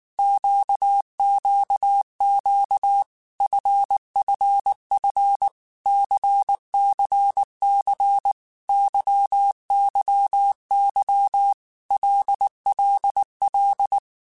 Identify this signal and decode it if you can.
Morse code, heard as a keyed tone, about 19 wpm, 800 Hz. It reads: QQQ FFF CCC YYY LLL
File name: Groups@18 / Groups@18 mp3 / LFCYQ@18